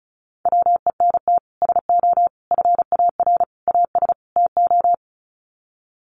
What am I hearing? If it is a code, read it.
Morse code, heard as a keyed tone, about 34 wpm, 700 Hz. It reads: WENT SO FAR AS TO
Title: Sets of 5 Words Encore 35wpm